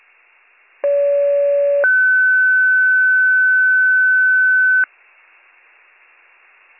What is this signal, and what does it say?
Two-tone page, heard as a one-second tone, then a three-second tone, about 570 then 1500 Hz. It is heard from 0.8 to 4.8 s.